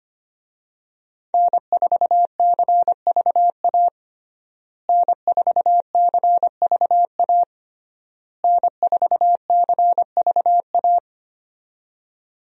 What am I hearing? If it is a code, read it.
Morse code, heard as a keyed tone, about 25 wpm, 700 Hz. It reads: N4CVA N4CVA N4CVA
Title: USA Call Signs - 5 Characters - Repeated 3x - 25wpm